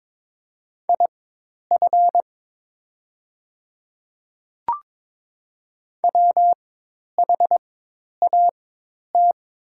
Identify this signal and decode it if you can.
Morse code, heard as a keyed tone, about 22 wpm, spaced as for 10 wpm, 700 Hz. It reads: IF WHAT
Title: 4x Character Spacing - Top 100 Words - 22wpm